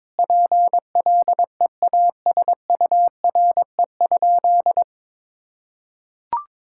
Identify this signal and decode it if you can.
Morse code, heard as a keyed tone, about 22 wpm, 700 Hz. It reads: PLEASURE?
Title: Sentences from Top 1600 Words 22wpm